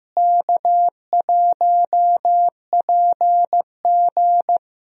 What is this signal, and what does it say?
Morse code, heard as a keyed tone, about 15 wpm, 700 Hz. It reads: K1PG
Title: USA Call Signs - 4 Characters - Repeated 3x - 15wpm